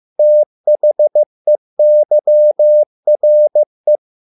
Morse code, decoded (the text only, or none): THEYRE